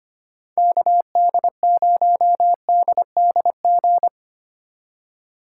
Morse code, heard as a keyed tone, about 31 wpm, 700 Hz. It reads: KD0DDG
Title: Speed Racing - USA 2x3 Call Signs 25wpm